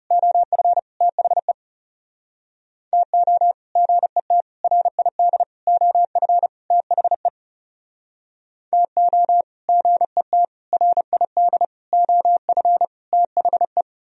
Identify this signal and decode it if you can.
Morse code, heard as a keyed tone, about 38 wpm, 700 Hz. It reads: OF THE TO GET RID OF THE TO GET RID OF THE
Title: Speed-Racing - Sets of 5 Words 30wpm